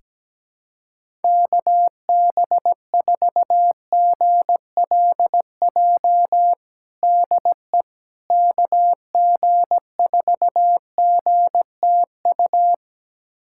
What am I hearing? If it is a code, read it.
Morse code, heard as a keyed tone, about 17 wpm, 700 Hz. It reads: KB4GLJ DE KG4GTU
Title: USA Callsign DE Callsign - 6 Characters - 17wpm